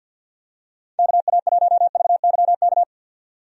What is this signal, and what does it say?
Morse code, heard as a keyed tone, about 47 wpm, 700 Hz. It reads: KA1VYK
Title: USA Call Signs - 6 Characters - 50wpm